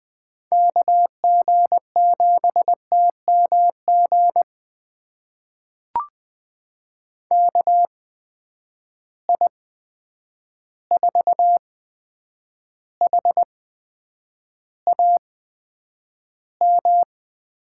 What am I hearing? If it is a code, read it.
Morse code, heard as a keyed tone, about 21 wpm, spaced as for 9 wpm, 700 Hz. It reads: KG7TMG KI4HAM
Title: USA Call Signs - 6 Characters - 8x Character Spacing - 20wpm